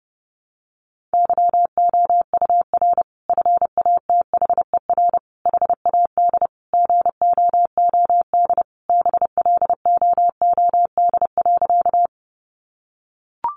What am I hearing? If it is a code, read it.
Morse code, heard as a keyed tone, about 30 wpm, 700 Hz. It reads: YOUR FATHER HAD GOOD BLOOD.
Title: Sentences from Top 1100 Words 30wpm